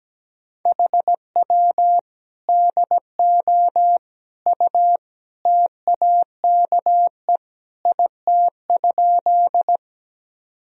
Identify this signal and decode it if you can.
Morse code, heard as a keyed tone, about 17 wpm, 700 Hz. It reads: HW DO U TAKE IT?
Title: Sentences from Top 200 Words and Abbreviations 17wpm